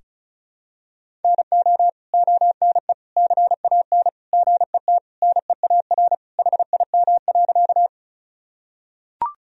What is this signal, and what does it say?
Morse code, heard as a keyed tone, about 36 wpm, 700 Hz. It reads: NO ONE CAN GET NEAR HIM.
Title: Sentences from Top 1000 Words - Review Part 1 - 35wpm